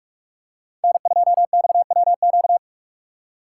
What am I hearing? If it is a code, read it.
Morse code, heard as a keyed tone, about 44 wpm, 700 Hz. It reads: N2XWQ